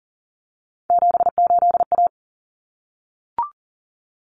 Morse code, heard as a keyed tone, about 40 wpm, 700 Hz. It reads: 78A
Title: Common Cut Numbers - 3-Digit 40wpm